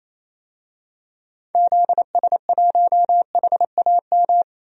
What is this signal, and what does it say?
Morse code, heard as a keyed tone, about 28 wpm, 700 Hz. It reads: ZS1HAM
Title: DXCC-100 Call Signs - 6 Characters - 28wpm